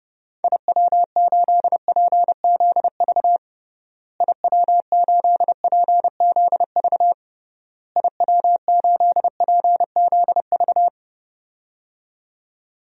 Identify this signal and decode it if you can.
Morse code, heard as a keyed tone, about 30 wpm, 700 Hz. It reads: IW8PZV IW8PZV IW8PZV
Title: DXCC-100 Call Signs - 6 Characters - Repeated 3x - 30wpm